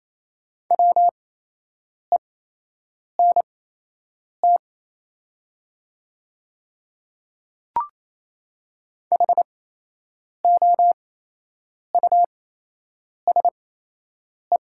Morse code, heard as a keyed tone, about 27 wpm, spaced as for 8 wpm, 700 Hz. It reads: WENT HOUSE
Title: 8x Character Spacing - Top 300 Words - 28wpm